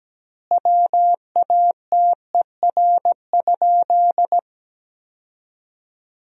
Morse code, heard as a keyed tone, about 17 wpm, 700 Hz. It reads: WATER?
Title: 4x Word-Spacing - Sentences from Top 200 Words - 17wpm